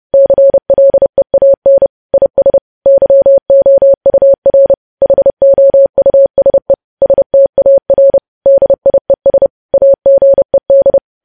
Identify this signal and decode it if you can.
Morse code, heard as a keyed tone, about 30 wpm, 550 Hz. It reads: CLEAN IS YOUR HOUSE STAR DIES AGED
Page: news headlines in morse code at 30 wpm